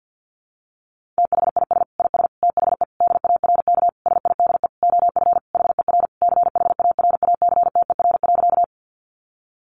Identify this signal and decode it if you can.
Morse code, heard as a keyed tone, about 50 wpm, 700 Hz. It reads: THIS IS THE DARK SIDE OF HER CHARACTER.
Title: Sentences from Top 1200 Words 50wpm